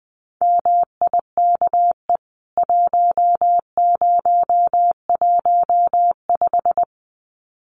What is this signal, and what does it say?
Morse code, heard as a keyed tone, about 20 wpm, 700 Hz. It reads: MIKE 1015
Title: CWT Exchange 20wpm